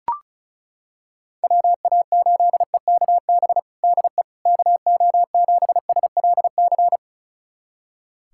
Morse code, heard as a keyed tone, about 36 wpm, 700 Hz. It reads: WA8EKB DE KO7SLC
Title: USA Callsign DE Callsign - 6 Characters - 35wpm